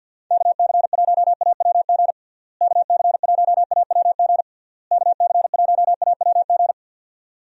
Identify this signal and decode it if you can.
Morse code, heard as a keyed tone, about 49 wpm, 700 Hz. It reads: KC1AWG KC1AWG KC1AWG
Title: USA Call Signs - 6 Characters - Repeated 3x - 50wpm